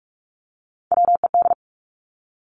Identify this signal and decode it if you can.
Morse code, heard as a keyed tone, about 46 wpm, 700 Hz. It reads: WED
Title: Three Challenging Letter-Number Combinations 45wpm